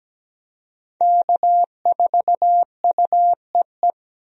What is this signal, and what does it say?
Morse code, heard as a keyed tone, about 17 wpm, 700 Hz. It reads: K4UEE
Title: USA Call Signs - 5 Characters - Repeated 3x - 17wpm